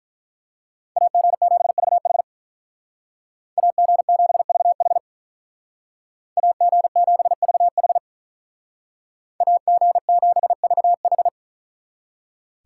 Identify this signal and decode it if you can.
Morse code, heard as a keyed tone, about 43 wpm, 700 Hz. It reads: AG7VH AG7VH AG7VH AG7VH
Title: Speed Racing - USA 2x2 Call Signs 35wpm